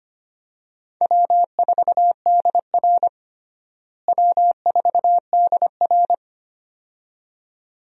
Morse code, heard as a keyed tone, about 25 wpm, 700 Hz. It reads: W4DR W4DR
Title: USA Call Signs - 4 Characters - Repeated 2x - 25wpm